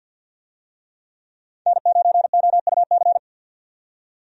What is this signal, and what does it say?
Morse code, heard as a keyed tone, about 46 wpm, 700 Hz. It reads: N9OUC
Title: USA Call Signs - 5 Characters - 50wpm